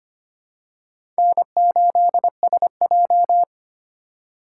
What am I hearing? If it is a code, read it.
Morse code, heard as a keyed tone, about 25 wpm, 700 Hz. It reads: N8SJ
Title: USA Call Signs - 4 Characters - 3x Character Spacing - 25wpm